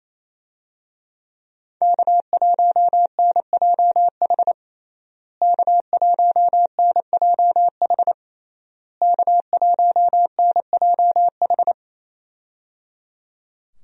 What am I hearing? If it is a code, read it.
Morse code, heard as a keyed tone, about 27 wpm, 700 Hz. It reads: K1NJH K1NJH K1NJH
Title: USA Call Signs - 5 Characters - Repeated 3x - 28wpm